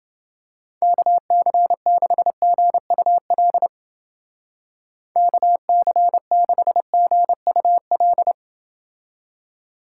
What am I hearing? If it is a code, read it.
Morse code, heard as a keyed tone, about 29 wpm, 700 Hz. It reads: KC6GUL KC6GUL